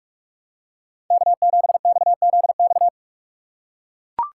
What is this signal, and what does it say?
Morse code, heard as a keyed tone, about 43 wpm, 700 Hz. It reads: K7XZX
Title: USA Call Signs - 4x Character Spacing - 45wpm